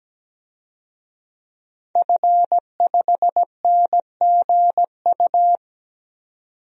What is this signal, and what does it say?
Morse code, heard as a keyed tone, about 17 wpm, 700 Hz. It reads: F5NGU
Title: DXCC-100 Call Signs - 5 Characters - 17wpm